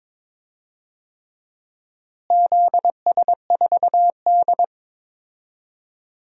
Morse code, heard as a keyed tone, about 22 wpm, 700 Hz. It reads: ZS4D
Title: DXCC-100 Call Signs - 4 Characters - Repeated 3x - 22wpm